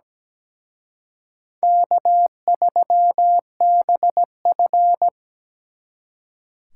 Morse code, heard as a keyed tone, about 20 wpm, 700 Hz. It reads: K3BF
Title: Speed Racing - USA 1x2 Call Signs 17wpm